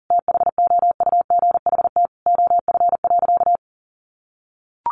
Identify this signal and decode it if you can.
Morse code, heard as a keyed tone, about 40 wpm, 700 Hz. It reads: THOUGHT OF.
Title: Sentences from Top 300 Words 40wpm